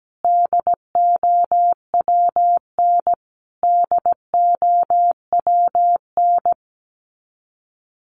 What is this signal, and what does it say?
Morse code, heard as a keyed tone, about 17 wpm, 700 Hz. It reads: DOWN DOWN
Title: Top 500 Words Repeated 17wpm